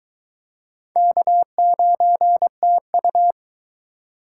Morse code, heard as a keyed tone, about 24 wpm, 700 Hz. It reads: K9TU